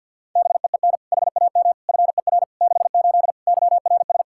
Morse code, heard as a keyed tone, about 46 wpm, 700 Hz. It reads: BEEN HAM FER 68 YRS
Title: Practice QSO - Exchange 3 - More Info - 50wpm